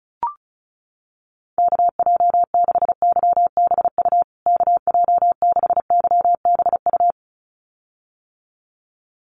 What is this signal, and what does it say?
Morse code, heard as a keyed tone, about 34 wpm, 700 Hz. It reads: KJ6YBU KJ6YBU
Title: USA Call Signs Repeated 2x 35wpm